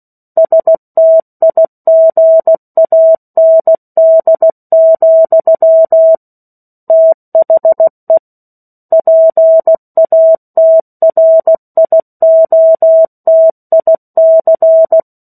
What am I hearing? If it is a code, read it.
Morse code, heard as a keyed tone, about 16 wpm, 650 Hz. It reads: STIGAND, THE PATRIOTIC
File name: 03-Chapter-03 16 WPM